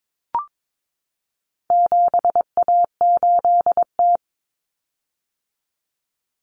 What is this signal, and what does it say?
Morse code, heard as a keyed tone, about 22 wpm, 700 Hz. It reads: 7A8T
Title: Common Cut Numbers - 4-Digit 22wpm